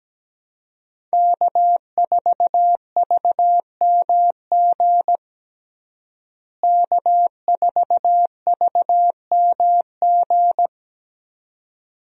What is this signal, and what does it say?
Morse code, heard as a keyed tone, about 17 wpm, 700 Hz. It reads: K4VMG K4VMG
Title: USA Call Signs - 5 Characters - Repeated 2x - 17wpm